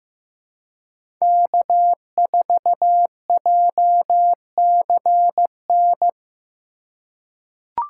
Morse code, heard as a keyed tone, about 15 wpm, 700 Hz. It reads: K4JCN
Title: USA Call Signs - 5 Characters - 4x Character Spacing - 15wpm